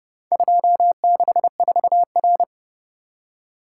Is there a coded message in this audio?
Yes: Morse code from 0.3 to 2.4 s, heard as a keyed tone, about 31 wpm, 700 Hz. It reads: 264R